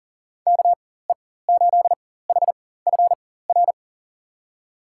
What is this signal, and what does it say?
Morse code, heard as a keyed tone, about 38 wpm, spaced as for 18 wpm, 700 Hz. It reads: KE8HFR
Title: USA Call Signs - 4x Character Spacing - 40wpm